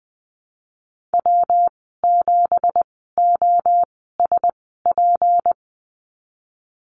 Morse code, heard as a keyed tone, about 20 wpm, spaced as for 15 wpm, 700 Hz. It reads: W7OSP
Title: 231 Lesson 37g - Letter x - Call Signs 20wpm